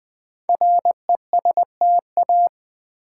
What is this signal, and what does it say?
Morse code, heard as a keyed tone, about 20 wpm, 700 Hz. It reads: RESTA
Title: Build A Word - 5 Letters At A Time - 10-Letter Words - 20wpm